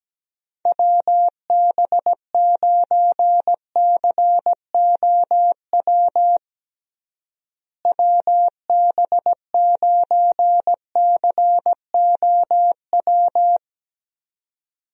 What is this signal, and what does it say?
Morse code, heard as a keyed tone, about 17 wpm, 700 Hz. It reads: WB9COW WB9COW
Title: USA Call Signs - 6 Characters - Repeated 2x - 17wpm